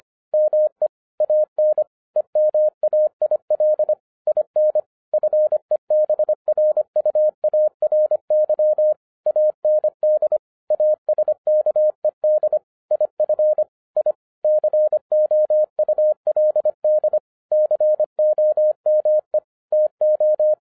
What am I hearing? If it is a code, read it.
Morse code, heard as a keyed tone, about 25 wpm, 610 Hz. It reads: ME AN EMAIL IN FEBRUARY AND ASKED IF I COULD COME TO